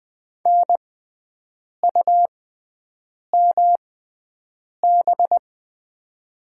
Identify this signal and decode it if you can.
Morse code, heard as a keyed tone, about 20 wpm, spaced as for 7 wpm, 700 Hz. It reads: NUMB
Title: 6x Character Spacing - Top 100 Words - 20wpm